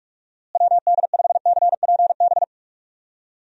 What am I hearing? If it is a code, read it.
Morse code, heard as a keyed tone, about 43 wpm, 700 Hz. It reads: WD5CPB
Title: USA Call Signs - 6 Characters - Repeated 2x - 45wpm